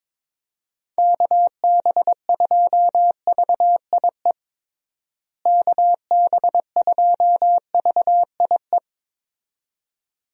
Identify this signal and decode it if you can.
Morse code, heard as a keyed tone, about 22 wpm, 700 Hz. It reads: KB2VIE KB2VIE
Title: USA Call Signs - 6 Characters - Repeated 2x - 22wpm